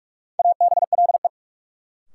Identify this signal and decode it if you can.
Morse code, heard as a keyed tone, about 46 wpm, 700 Hz. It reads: ABLE
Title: Sentences from Top 700 Words - New Word Drill - 45wpm